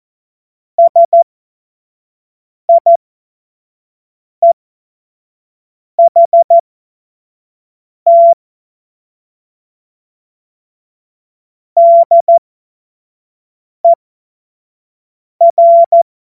Morse code, heard as a keyed tone, about 14 wpm, spaced as for 5 wpm, 680 Hz. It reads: SIEHT DER
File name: Morsekurs / Witze / Witz 07. September 2021, 25 F70